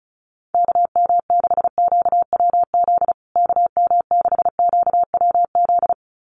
Morse code, heard as a keyed tone, about 34 wpm, 700 Hz. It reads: KM6QWZ KM6QWZ
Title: USA Call Signs Repeated 2x 35wpm